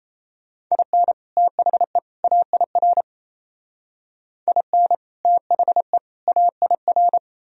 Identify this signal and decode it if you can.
Morse code, heard as a keyed tone, about 37 wpm, 700 Hz. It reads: IN THE AIR IN THE AIR